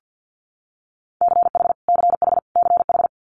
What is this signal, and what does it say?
Morse code, heard as a keyed tone, about 50 wpm, 700 Hz. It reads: CH CH CH
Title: Multi-letter Phonemes Repeated 3x 50wpm